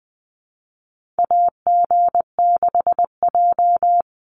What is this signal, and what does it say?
Morse code, heard as a keyed tone, about 20 wpm, 700 Hz. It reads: AG6J
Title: USA Call Signs 20wpm